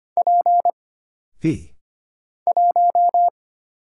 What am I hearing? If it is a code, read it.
Morse code, heard as a keyed tone, about 25 wpm, 700 Hz. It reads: P 1
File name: Single Character (Letter:Number:Punctuation:Prosign) - Mind-Melt - 25wpm